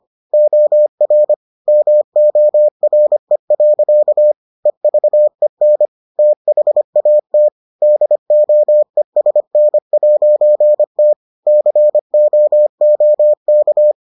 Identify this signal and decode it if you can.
Morse code, heard as a keyed tone, about 25 wpm, 600 Hz. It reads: OR MORE. EVEN THAT DOESN'T COOK